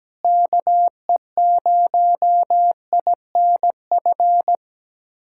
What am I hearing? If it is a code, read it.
Morse code, heard as a keyed tone, about 17 wpm, 700 Hz. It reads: KE0INF
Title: USA Callsign DE Callsign 17wpm